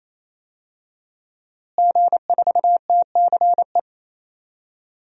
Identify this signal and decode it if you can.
Morse code, heard as a keyed tone, about 28 wpm, 700 Hz. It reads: G4TCE
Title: DXCC-100 Call Signs - 5 Characters - Repeated 3x - 28wpm